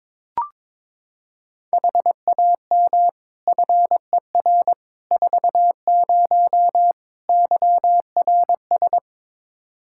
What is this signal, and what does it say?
Morse code, heard as a keyed tone, about 22 wpm, 700 Hz. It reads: HAM FER 40 YRS
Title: Practice QSO - Exchange 3 - More Info - 22wpm